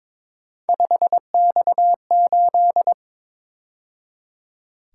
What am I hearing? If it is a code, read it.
Morse code, heard as a keyed tone, about 26 wpm, 700 Hz. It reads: 5X8
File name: Speed-Racing - Three Letter-Number Combinations - 22wpm